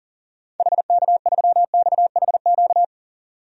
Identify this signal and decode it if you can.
Morse code, heard as a keyed tone, about 40 wpm, 700 Hz. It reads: HK3XHQ
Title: DXCC-100 Call Signs - 6 Characters - 40wpm